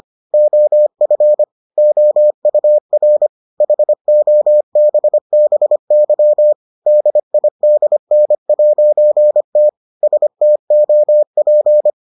Morse code, heard as a keyed tone, about 25 wpm, 600 Hz. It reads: OF OUR HOBBY DIDN'T STOP